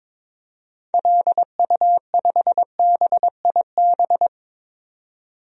Morse code, heard as a keyed tone, about 22 wpm, 700 Hz. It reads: LU5BIB
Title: DXCC-100 Call Signs - 6 Characters - 22wpm